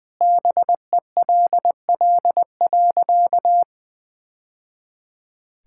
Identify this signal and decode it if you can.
Morse code, heard as a keyed tone, about 21 wpm, 700 Hz. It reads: BELL.
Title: Sentences with 2-Word Phrases - 20wpm